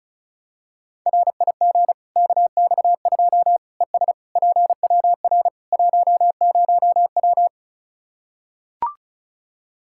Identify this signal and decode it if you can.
Morse code, heard as a keyed tone, about 34 wpm, 700 Hz. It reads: RIG KX2 ES PWR 10W
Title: K1USN SST QSO Practice 35wpm